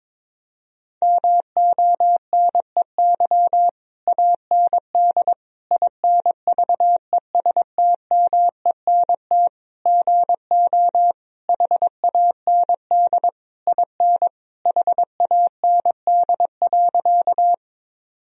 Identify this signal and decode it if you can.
Morse code, heard as a keyed tone, about 22 wpm, 700 Hz. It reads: MONEY AND INVESTMENT GO HAND IN HAND.
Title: Sentences from Top 1000 Words - Review Part 4 - 22wpm